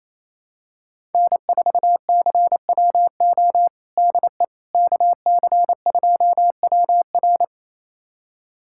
Morse code, heard as a keyed tone, about 28 wpm, 700 Hz. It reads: N4CWO DE KC2WR
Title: USA Callsign DE Callsign - 5 Characters - 28wpm